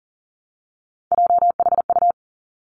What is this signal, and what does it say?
Morse code, heard as a keyed tone, about 42 wpm, 700 Hz. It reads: JHU
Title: Three Challenging Letter-Number Combinations 40wpm